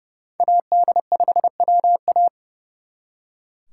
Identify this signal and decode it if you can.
Morse code, heard as a keyed tone, about 37 wpm, 700 Hz. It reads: AD5WA
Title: Speed Racing - USA 2x2 Call Signs 30wpm